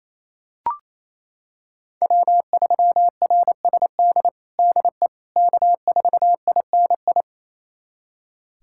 Morse code, heard as a keyed tone, about 28 wpm, 700 Hz. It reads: W3RSD DE K4INI
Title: USA Callsign DE Callsign - 5 Characters - 28wpm